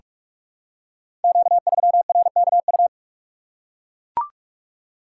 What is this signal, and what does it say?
Morse code, heard as a keyed tone, about 44 wpm, 700 Hz. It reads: Q3RKU
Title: Code Groups - 5 Letter:Numbers - 45wpm